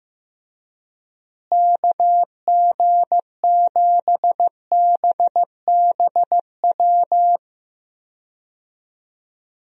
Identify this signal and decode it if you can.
Morse code, heard as a keyed tone, about 15 wpm, 700 Hz. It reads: KG7BBW
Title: USA Call Signs - 6 Characters - Repeated 3x - 15wpm